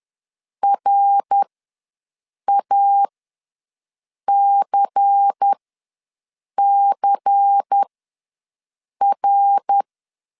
Morse code, heard as a keyed tone, about 10 wpm, spaced as for 6 wpm, 800 Hz. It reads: RACCR
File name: LEZIONI / 5 LBVC / 5 LBVC ws 10wpm 3733